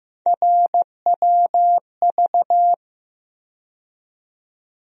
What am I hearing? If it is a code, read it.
Morse code, heard as a keyed tone, about 15 wpm, 700 Hz. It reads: RWV
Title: USA Call Sign Postfixes - 3 Letters 15wpm